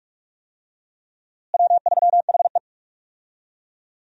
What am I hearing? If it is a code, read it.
Morse code, heard as a keyed tone, about 41 wpm, 700 Hz. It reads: W3HE
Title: USA Call Signs - 4 Characters - Repeated 3x - 45wpm